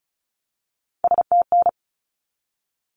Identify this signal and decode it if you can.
Morse code, heard as a keyed tone, about 34 wpm, 700 Hz. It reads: STN
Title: Abbreviations 35wpm